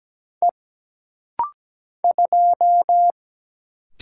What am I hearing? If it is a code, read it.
Morse code, heard as a keyed tone, about 17 wpm, 700 Hz. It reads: E 2
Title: Single Letter-Number - Rapid-Fire 17wpm